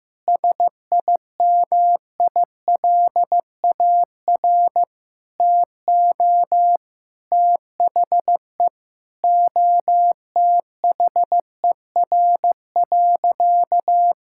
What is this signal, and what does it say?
Morse code, heard as a keyed tone, about 15 wpm, 700 Hz. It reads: SIMILAR TO THE OTHER.
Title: Sentences from Top 1000 Words - Review Part 5 - 15wpm